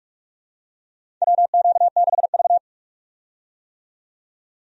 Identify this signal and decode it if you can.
Morse code, heard as a keyed tone, about 46 wpm, 700 Hz. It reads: WQ6V
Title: USA Call Signs - 4 Characters - 45wpm